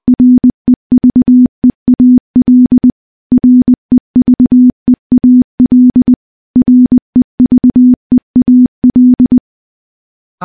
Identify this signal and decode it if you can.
Morse code, heard as a keyed tone, about 20 wpm, 250 Hz. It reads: REVEAL REVEAL REVEAL
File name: morse-training-20WPM